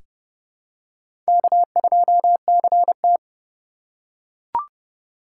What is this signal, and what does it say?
Morse code, heard as a keyed tone, about 30 wpm, 700 Hz. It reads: K2CT
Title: USA Call Signs - 4 Characters - 4x Character Spacing - 30wpm